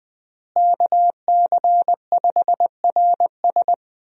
Morse code, heard as a keyed tone, about 20 wpm, 700 Hz. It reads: KC5RS
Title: USA Call Signs - 5 Characters - Repeated 2x - 20wpm